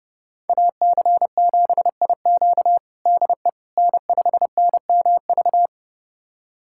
Morse code, heard as a keyed tone, about 32 wpm, 700 Hz. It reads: AC7IQ DE N5NMV
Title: USA Callsign DE Callsign - 5 Characters - 30wpm